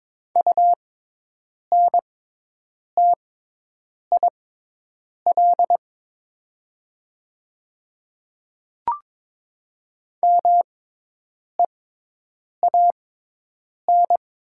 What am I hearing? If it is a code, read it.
Morse code, heard as a keyed tone, about 22 wpm, spaced as for 8 wpm, 700 Hz. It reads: UNTIL MEAN
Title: 6x Character Spacing - Top 300 Words - 22wpm